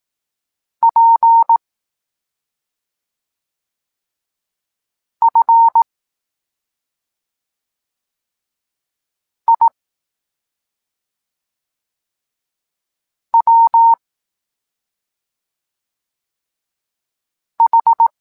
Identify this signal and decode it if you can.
Morse code, heard as a keyed tone, about 18 wpm, 920 Hz. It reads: P F I W H
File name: Tekst-26 1